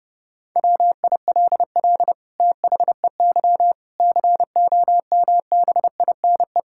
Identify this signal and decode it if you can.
Morse code, heard as a keyed tone, about 32 wpm, 700 Hz. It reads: WILL THEY COMBINE
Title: Sentences from Top 1600 Words 30wpm